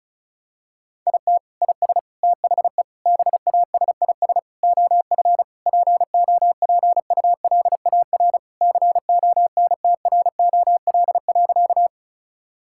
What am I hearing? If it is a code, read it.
Morse code, heard as a keyed tone, about 36 wpm, 700 Hz. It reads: IT IS THE BASIS OF POPULAR CONTROL.
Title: Sentences from Top 1000 Words - Review Part 5 - 35wpm